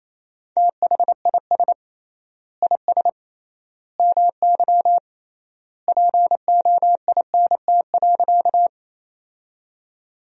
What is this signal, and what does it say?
Morse code, heard as a keyed tone, about 28 wpm, 700 Hz. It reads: THIS IS MY POINT.
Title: 3x Word-Spacing - Sentences from Top 200 Words - 28wpm